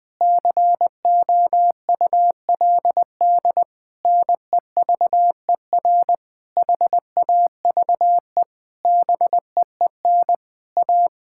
Morse code, heard as a keyed tone, about 20 wpm, 700 Hz. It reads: COULD NEVER HAVE BEEN A
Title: Sentences from Top 1000 Words - Review Part 6 - 20wpm